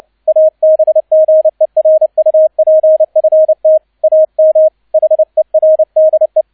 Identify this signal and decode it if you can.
Morse code, heard as a keyed tone, about 29 wpm, 630 Hz. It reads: ABGERUPFT AM HERDE
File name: 002 Zweiter Streich 30wpm